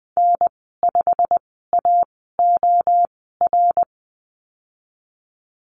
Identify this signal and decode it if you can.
Morse code, heard as a keyed tone, about 20 wpm, spaced as for 15 wpm, 700 Hz. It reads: N5AOR
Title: 140 Lesson 23g - Letter f - Call Signs 20wpm